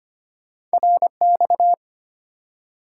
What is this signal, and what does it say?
Morse code, heard as a keyed tone, about 25 wpm, 690 Hz. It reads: RX
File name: Top 90 QSO Elements - 6x Character Spacing - 25wpm